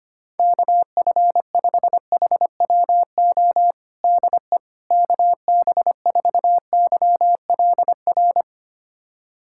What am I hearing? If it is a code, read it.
Morse code, heard as a keyed tone, about 25 wpm, 700 Hz. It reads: KF5HWO DE KB4YLR
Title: USA Callsign DE Callsign - 6 Characters - 25wpm